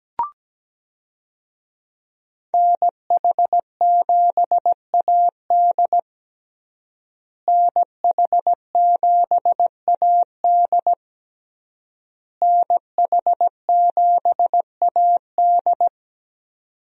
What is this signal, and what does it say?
Morse code, heard as a keyed tone, about 17 wpm, 700 Hz. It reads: NH7AD NH7AD NH7AD
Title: USA Call Signs - 5 Characters - Repeated 3x - 17wpm